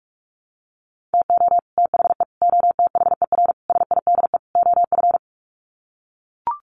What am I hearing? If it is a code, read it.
Morse code, heard as a keyed tone, about 44 wpm, 700 Hz. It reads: TO THE OTHER SIDE OF
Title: Sets of 5 Words Encore 45wpm